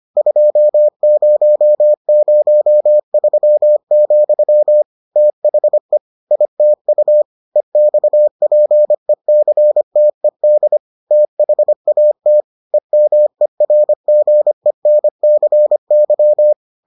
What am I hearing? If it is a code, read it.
Morse code, heard as a keyed tone, about 25 wpm, 600 Hz. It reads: 2003, THE ITU EXPECTED THAT EMERGENCY